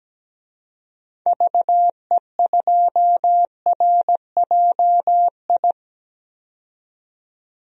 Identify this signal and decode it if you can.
Morse code, heard as a keyed tone, about 17 wpm, 700 Hz. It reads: VE2RJI
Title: DXCC-100 Call Signs - 6 Characters - 17wpm